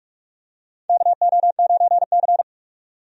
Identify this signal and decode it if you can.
Morse code, heard as a keyed tone, about 46 wpm, 700 Hz. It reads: KO9C
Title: USA Call Signs - 4 Characters - 45wpm